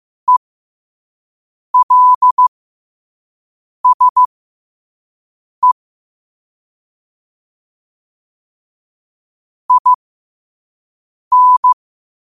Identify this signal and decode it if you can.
Morse code, heard as a keyed tone, about 15 wpm, spaced as for 5 wpm, 1000 Hz. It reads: ELSE IN
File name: qotdfiles / 5wpm